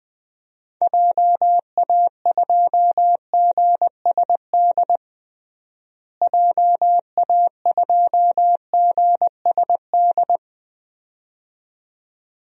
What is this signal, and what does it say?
Morse code, heard as a keyed tone, about 20 wpm, 700 Hz. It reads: JA2GSD JA2GSD
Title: DXCC-100 Call Signs - 6 Characters - Repeated 2x - 20wpm